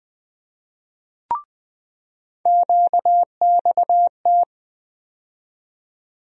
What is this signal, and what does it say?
Morse code, heard as a keyed tone, about 21 wpm, 700 Hz. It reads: QXT
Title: USA Call Sign Postfixes - 3 Letters 20wpm